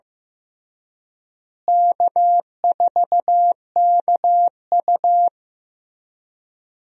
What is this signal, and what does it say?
Morse code, heard as a keyed tone, about 15 wpm, 700 Hz. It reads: K4KU
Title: USA Call Signs - 4 Characters - Repeated 3x - 15wpm